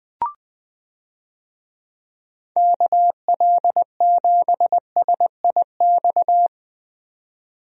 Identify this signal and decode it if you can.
Morse code, heard as a keyed tone, about 20 wpm, 700 Hz. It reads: KL7SIX
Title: DXCC-100 Call Signs - 6 Characters - 20wpm